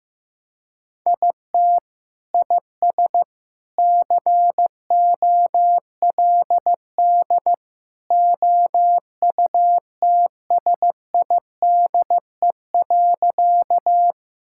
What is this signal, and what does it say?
Morse code, heard as a keyed tone, about 16 wpm, 700 Hz. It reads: IT IS COLD OUTSIDE.
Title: Sentences from Top 1000 Words - Review Part 6 - 15wpm